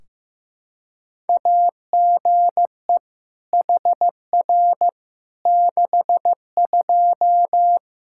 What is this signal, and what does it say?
Morse code, heard as a keyed tone, about 15 wpm, 700 Hz. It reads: AGE HR 62
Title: Practice QSO - Exchange 3 - More Info - 15wpm